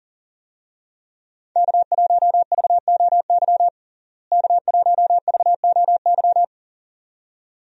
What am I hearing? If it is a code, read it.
Morse code, heard as a keyed tone, about 40 wpm, 700 Hz. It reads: K1VOY K1VOY
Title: USA Call Signs - 5 Characters - Repeated 2x - 40wpm